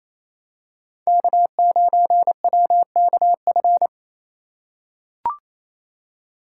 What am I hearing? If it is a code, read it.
Morse code, heard as a keyed tone, about 28 wpm, 700 Hz. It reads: K9WKF
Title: USA Call Signs - 5 Characters - 2x Character Spacing - 28wpm